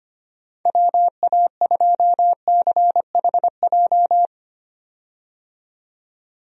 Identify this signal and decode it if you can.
Morse code, heard as a keyed tone, about 25 wpm, 700 Hz. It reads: WA2CHJ
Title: USA Call Signs - 6 Characters - 25wpm